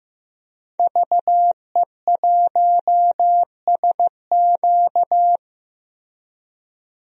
Morse code, heard as a keyed tone, about 16 wpm, 700 Hz. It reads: VE1SQ
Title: DXCC-100 Call Signs - 5 Characters - 15wpm